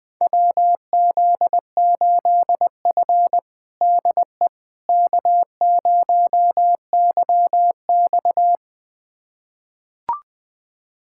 Morse code, heard as a keyed tone, about 20 wpm, 700 Hz. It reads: WZ8F DE K0YX
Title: USA Callsign DE Callsign - 4 Characters - 20wpm